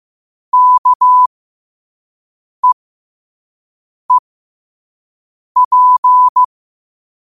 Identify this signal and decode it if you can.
Morse code, heard as a keyed tone, about 15 wpm, spaced as for 5 wpm, 1000 Hz. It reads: KEEP
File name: qotdfiles / 5wpm